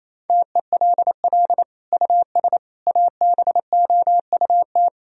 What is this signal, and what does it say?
Morse code, heard as a keyed tone, about 28 wpm, 700 Hz. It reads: TELL US ABOUT
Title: Sentences from Top 1000 Words - Review Part 3 - 28wpm